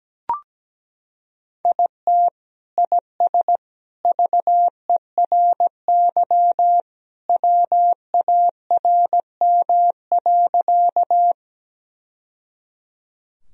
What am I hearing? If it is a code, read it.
Morse code, heard as a keyed tone, about 17 wpm, 700 Hz. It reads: IT IS VERY WARM.
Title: Sentences from Top 1000 Words - Review Part 2 - 17wpm